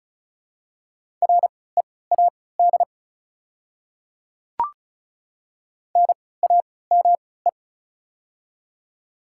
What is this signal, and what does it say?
Morse code, heard as a keyed tone, about 36 wpm, spaced as for 20 wpm, 700 Hz. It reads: READ NAME
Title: 3x Character Spacing - Top 200 Words - 35wpm